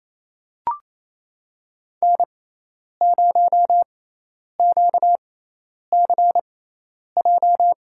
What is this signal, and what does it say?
Morse code, heard as a keyed tone, about 27 wpm, spaced as for 10 wpm, 700 Hz. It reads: N0QCJ
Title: USA Call Signs - 6x Character Spacing - 28wpm